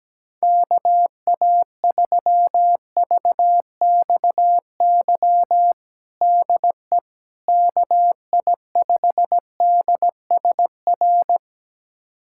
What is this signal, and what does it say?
Morse code, heard as a keyed tone, about 17 wpm, 700 Hz. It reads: KA3VXY DE KI5DSR
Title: USA Callsign DE Callsign 17wpm